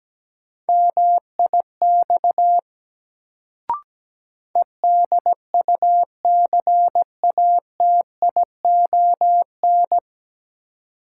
Morse code, heard as a keyed tone, about 17 wpm, 700 Hz. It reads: MIX EDUCATION
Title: Sentences from Top 1100 Words - New Word Drill Alt - 17wpm